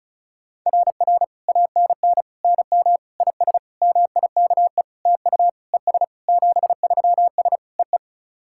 Morse code, heard as a keyed tone, about 36 wpm, 700 Hz. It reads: RR ANN NM IS MIKE TU ES 73S EE
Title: SOTA QSO Practice 35wpm